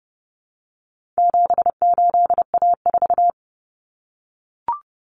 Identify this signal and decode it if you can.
Morse code, heard as a keyed tone, about 30 wpm, 700 Hz. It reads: 78A4
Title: Common Cut Numbers - 4-Digit 30wpm